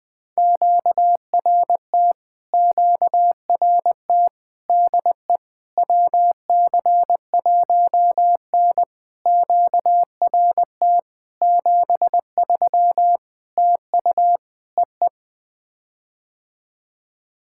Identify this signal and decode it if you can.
Morse code, heard as a keyed tone, about 21 wpm, 700 Hz. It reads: QRT QRT DE WC1N QRT 73 TU EE
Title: POTA QSO Practice - 20wpm